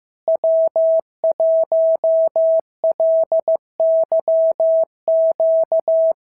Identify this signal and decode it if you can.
Morse code, heard as a keyed tone, about 15 wpm, 660 Hz. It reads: W1LYQ
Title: SOTA QSO Practice 15wpm